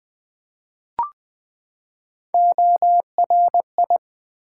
Morse code, heard as a keyed tone, about 20 wpm, 700 Hz. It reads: ORI
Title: Build A Word - 3 Letters At A Time - 8-Letter Words - 20wpm